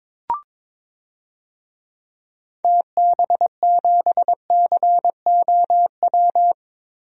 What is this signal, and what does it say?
Morse code, heard as a keyed tone, about 22 wpm, 700 Hz. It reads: TB7COW
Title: DXCC-100 Call Signs - 6 Characters - Repeated 2x - 22wpm